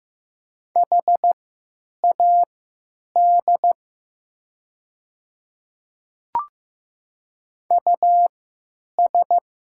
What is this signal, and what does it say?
Morse code, heard as a keyed tone, about 15 wpm, spaced as for 9 wpm, 700 Hz. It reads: HAD US
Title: 3x Character Spacing - Top 200 Words - 15wpm